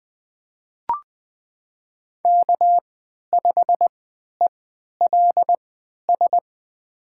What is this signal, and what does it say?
Morse code, heard as a keyed tone, about 20 wpm, spaced as for 11 wpm, 700 Hz. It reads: K5ELS
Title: USA Call Signs - 5 Characters - 3x Character Spacing - 20wpm